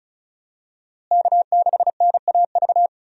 Morse code, heard as a keyed tone, about 34 wpm, 700 Hz. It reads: K6NAV
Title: USA Call Signs - 5 Characters - Repeated 3x - 35wpm